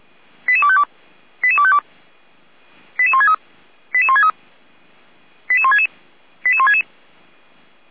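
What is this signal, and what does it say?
Five-tone selective call (ZVEI), heard as quick runs of five tones, tones of about 70 ms. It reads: 88252 88252 88163 88163 88166 88166
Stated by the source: ZVEI five-tone selective call received by SDR.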